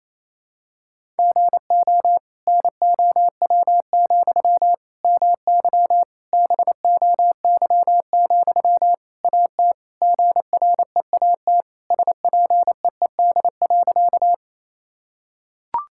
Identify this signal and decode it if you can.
Morse code, heard as a keyed tone, about 28 wpm, 700 Hz. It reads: GO NOW, MY BOY, AT GREAT SPEED.
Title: Sentences from Top 1000 Words - Review Part 5 - 28wpm